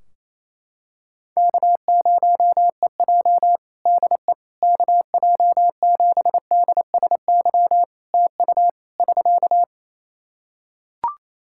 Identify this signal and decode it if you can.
Morse code, heard as a keyed tone, about 28 wpm, 700 Hz. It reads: K0EJ DE KJ7DSY TU <SK>
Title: Practice QSO - Exchange 4 - Ending - 28wpm